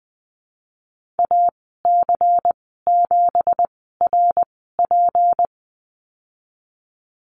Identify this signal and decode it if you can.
Morse code, heard as a keyed tone, about 20 wpm, spaced as for 15 wpm, 700 Hz. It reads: AC7RP
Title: 179 Lesson 29g - Slash - Call Signs 20wpm